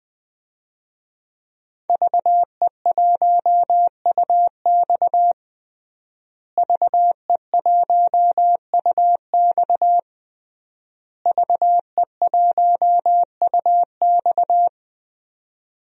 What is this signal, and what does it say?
Morse code, heard as a keyed tone, about 20 wpm, 700 Hz. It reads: VE1UX VE1UX VE1UX
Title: DXCC-100 Call Signs - 5 Characters - Repeated 3x - 20wpm